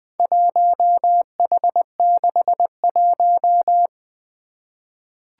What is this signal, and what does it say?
Morse code, heard as a keyed tone, about 20 wpm, 700 Hz. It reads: 1H61
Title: Four Challenging Letter-Number Combinations 20wpm